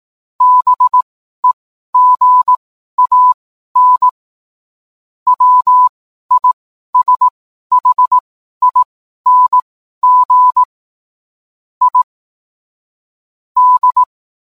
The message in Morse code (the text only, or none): BEGAN WISHING I D